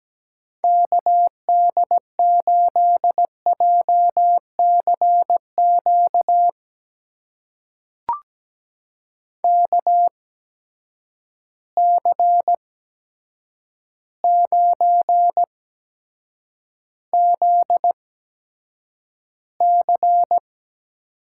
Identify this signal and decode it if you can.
Morse code, heard as a keyed tone, about 17 wpm, spaced as for 8 wpm, 700 Hz. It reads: KD8JCQ KC9ZC
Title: USA Call Signs - 6 Characters - 8x Character Spacing - 17wpm